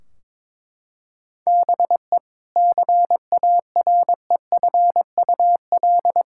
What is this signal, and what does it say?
Morse code, heard as a keyed tone, about 22 wpm, 700 Hz. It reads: BE CAREFUL
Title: Sentences from Top 1600 Words 22wpm